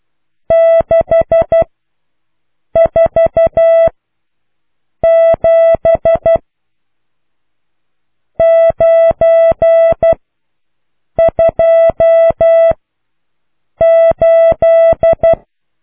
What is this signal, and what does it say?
Morse code, heard as a keyed tone, about 12 wpm, 660 Hz. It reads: 6 4 7 9 2 8